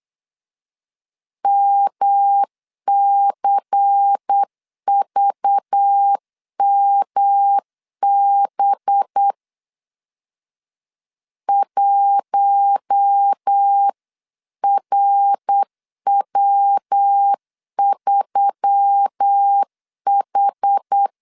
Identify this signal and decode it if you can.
Morse code, heard as a keyed tone, about 8 wpm, 800 Hz. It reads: MCVMB 1RW3H
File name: LEZIONI / 6 PKX0 / 6 PKX 8wpm 3041